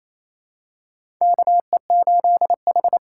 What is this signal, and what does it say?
Morse code, heard as a keyed tone, about 27 wpm, 700 Hz. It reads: KE8H
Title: USA Call Signs - 6 Characters - 8x Character Spacing - 28wpm